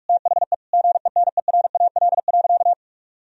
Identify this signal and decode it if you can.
Morse code, heard as a keyed tone, about 46 wpm, 700 Hz. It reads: THE GENERAL.
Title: Sentences from Top 1000 Words - Review Part 3 - 45wpm